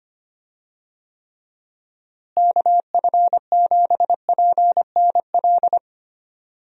Morse code, heard as a keyed tone, about 25 wpm, 700 Hz. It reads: KF7PNL
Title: USA Call Signs - 6 Characters - Repeated 2x - 25wpm